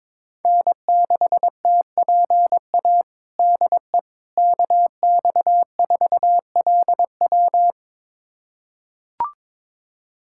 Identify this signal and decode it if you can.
Morse code, heard as a keyed tone, about 21 wpm, 700 Hz. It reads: N6TPA DE KX4LW
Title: USA Callsign DE Callsign - 5 Characters - 22wpm